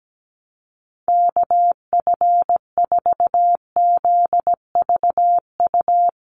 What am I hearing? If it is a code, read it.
Morse code, heard as a keyed tone, about 17 wpm, 700 Hz. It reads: KF4ZVU
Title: USA Call Signs 17wpm